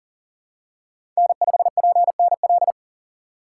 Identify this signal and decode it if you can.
Morse code, heard as a keyed tone, about 40 wpm, 700 Hz. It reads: N5PNL
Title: USA Call Signs - 2x Character Spacing - 40wpm